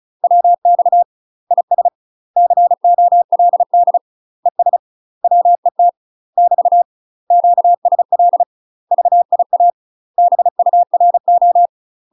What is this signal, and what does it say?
Morse code, heard as a keyed tone, about 36 wpm, 700 Hz. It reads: WX IS COLD ES WET <BT> QSL VIA BURO
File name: UK MASTER QSOs TRAINING / 35WPM / UK MASTER QSOs 0010